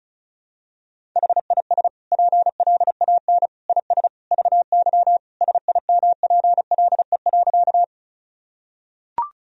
Morse code, heard as a keyed tone, about 36 wpm, 700 Hz. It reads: HIS PLAN IS VY SIMPLE.
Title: Sentences from Top 500 Words and Abbreviations 35wpm